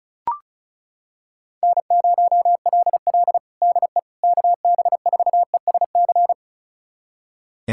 Morse code, heard as a keyed tone, about 36 wpm, 700 Hz. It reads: N0LL DE KB4ESC
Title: USA Callsign DE Callsign 35wpm